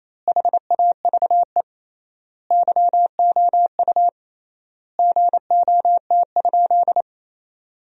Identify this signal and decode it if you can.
Morse code, heard as a keyed tone, about 28 wpm, 700 Hz. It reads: HAVE YOU GOT?
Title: 3x Word-Spacing - Sentences from Top 300 Words - 28wpm